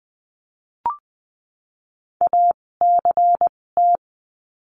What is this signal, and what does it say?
Morse code, heard as a keyed tone, about 20 wpm, spaced as for 16 wpm, 700 Hz. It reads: ACT
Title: 256 Lesson 42a - Words 12-20wpm